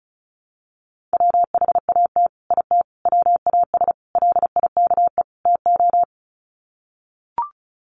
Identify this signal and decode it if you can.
Morse code, heard as a keyed tone, about 34 wpm, 700 Hz. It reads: WHAT IT WAS LIKE TO
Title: Sets of 5 Words Encore 35wpm